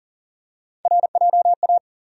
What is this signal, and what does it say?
Morse code, heard as a keyed tone, about 37 wpm, 700 Hz. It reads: RJA
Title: USA Call Sign Postfixes - 3 Letters 40wpm